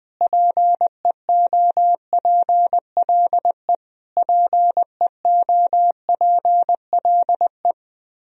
Morse code, heard as a keyed tone, about 21 wpm, 700 Hz. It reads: PEOPLE PEOPLE
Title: Top 100 Words Repeated 20wpm